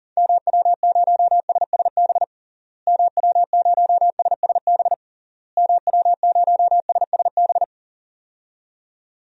Morse code, heard as a keyed tone, about 40 wpm, 700 Hz. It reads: MW0SSB MW0SSB MW0SSB
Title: DXCC-100 Call Signs - 6 Characters - Repeated 3x - 40wpm